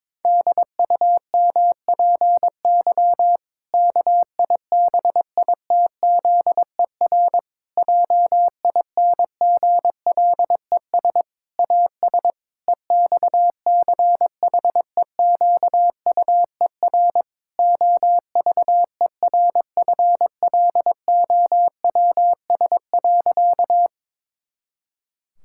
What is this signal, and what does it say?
Morse code, heard as a keyed tone, about 22 wpm, 700 Hz. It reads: DUMPY KIBITZER JINGLES AS EXCHEQUER OVERFLOWS.
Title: Pangram Challenge v2 - 22wpm